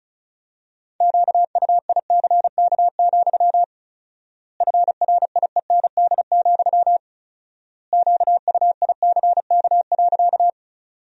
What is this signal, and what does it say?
Morse code, heard as a keyed tone, about 36 wpm, 700 Hz. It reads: QUICK, FRIEND, QUICK.
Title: 4x Word-Spacing - Sentences from Top 500 Words - 35wpm